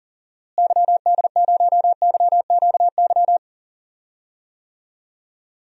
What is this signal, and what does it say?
Morse code, heard as a keyed tone, about 39 wpm, 700 Hz. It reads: YD0YQY
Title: DXCC-100 Call Signs - 6 Characters - Repeated 2x - 40wpm